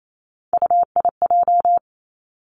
Morse code, heard as a keyed tone, about 28 wpm, 700 Hz. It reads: UIJ
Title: Three Challenging Letter-Number Combinations 28wpm